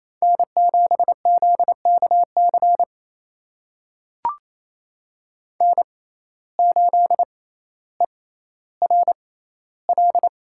Morse code, heard as a keyed tone, about 27 wpm, spaced as for 14 wpm, 700 Hz. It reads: N7ZKC N8ERL
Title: USA Call Signs - 5 Characters - 6x Character Spacing - 28wpm